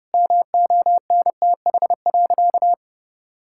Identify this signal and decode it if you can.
Morse code, heard as a keyed tone, about 30 wpm, 700 Hz. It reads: MONTH.
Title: Sentences from Top 1000 Words - Review Part 5 - 30wpm